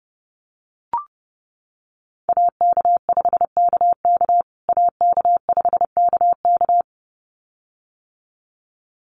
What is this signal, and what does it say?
Morse code, heard as a keyed tone, about 30 wpm, 700 Hz. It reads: AK5KK AK5KK
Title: USA Call Signs Repeated 2x 30wpm